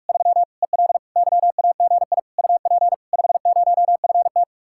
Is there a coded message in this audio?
Yes: Morse code from 0.1 to 4.4 s, heard as a keyed tone, about 50 wpm, 700 Hz. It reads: 3 EL YAGI UP 50FT